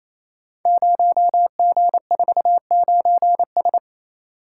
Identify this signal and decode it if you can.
Morse code, heard as a keyed tone, about 27 wpm, 700 Hz. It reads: 0G49S
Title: Code Groups - 5 Letter:Numbers - 28wpm